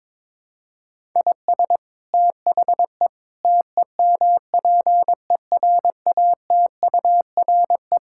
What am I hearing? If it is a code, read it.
Morse code, heard as a keyed tone, about 22 wpm, 700 Hz. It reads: IS THE TEMPERATURE
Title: Sentences from Top 1000 Words - Review Part 6 - 22wpm